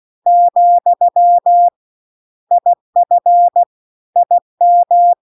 Morse code, heard as a keyed tone, about 16 wpm, 700 Hz. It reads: , IF IM